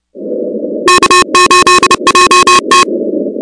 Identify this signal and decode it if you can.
Morse code, heard as a keyed tone, about 29 wpm, 1080 Hz, with noise. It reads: K8JT
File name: contests / 2019-cw-wpx-kt5j-k5pi- -2330 0-16